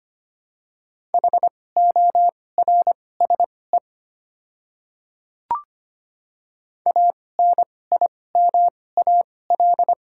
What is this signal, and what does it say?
Morse code, heard as a keyed tone, about 25 wpm, spaced as for 18 wpm, 700 Hz. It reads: HORSE ANIMAL
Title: 2x Character Spacing - Top 300 Words - 25wpm